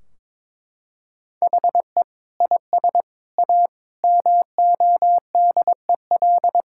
Morse code, heard as a keyed tone, about 21 wpm, 700 Hz. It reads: HE IS A MODEL
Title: Sentences from Top 1600 Words 22wpm